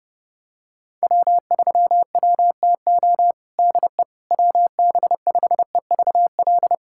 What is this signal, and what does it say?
Morse code, heard as a keyed tone, about 30 wpm, 700 Hz. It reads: W3WTO DE WB5EVL
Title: USA Callsign DE Callsign 30wpm